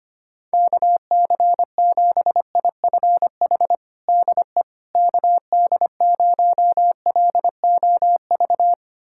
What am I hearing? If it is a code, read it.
Morse code, heard as a keyed tone, about 25 wpm, 700 Hz. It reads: KC7IFH DE KD0LOV
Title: USA Callsign DE Callsign 25wpm